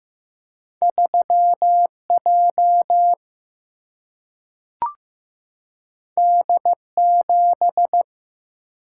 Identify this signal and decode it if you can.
Morse code, heard as a keyed tone, about 15 wpm, 700 Hz. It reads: 3 J D 7
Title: Single-Letter Single-Number 15wpm